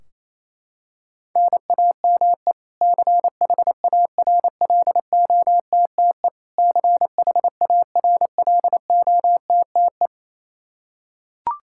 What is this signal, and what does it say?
Morse code, heard as a keyed tone, about 28 wpm, 700 Hz. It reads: NAME CHARLOTTE CHARLOTTE
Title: Practice QSO - Exchange 1 - Beginning - 28wpm